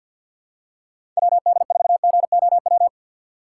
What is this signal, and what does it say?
Morse code, heard as a keyed tone, about 52 wpm, 700 Hz. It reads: WD4GOW
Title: USA Call Signs - 6 Characters - 6x Character Spacing - 50wpm